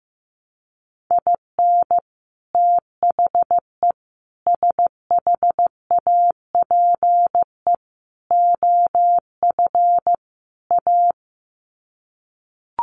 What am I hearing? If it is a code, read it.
Morse code, heard as a keyed tone, about 15 wpm, 700 Hz. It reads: IN THE SHAPE OF A
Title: Sets of 5 Words Encore 15wpm